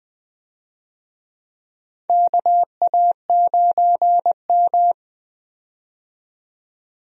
Morse code, heard as a keyed tone, about 20 wpm, 700 Hz. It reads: KA9M
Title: USA Call Signs - 4 Characters - 20wpm